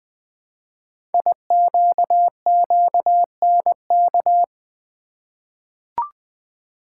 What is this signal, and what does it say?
Morse code, heard as a keyed tone, about 20 wpm, 700 Hz. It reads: IQQNK
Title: Code Groups - 5 Letters - 20wpm